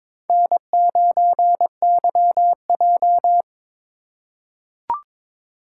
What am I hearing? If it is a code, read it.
Morse code, heard as a keyed tone, about 22 wpm, 700 Hz. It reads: N9YJ
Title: USA Call Signs - 4 Characters - 3x Character Spacing - 22wpm